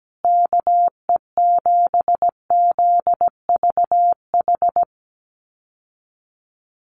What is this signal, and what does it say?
Morse code, heard as a keyed tone, about 17 wpm, 700 Hz. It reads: KE7ZVH
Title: USA Call Signs Repeated 3x 17wpm